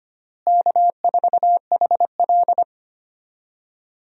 Morse code, heard as a keyed tone, about 25 wpm, 700 Hz. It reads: K4HL
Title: USA Call Signs - 4 Characters - Repeated 3x - 25wpm